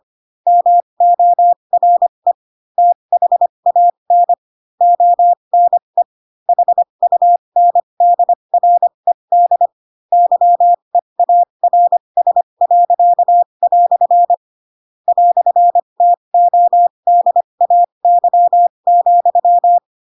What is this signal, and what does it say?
Morse code, heard as a keyed tone, about 25 wpm, spaced as for 22 wpm, 700 Hz. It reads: MORE THAN ONE HUNDRED YEARS." "TODAY,
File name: Pinocchio / Pinocchio ch4 18 25 ik1hge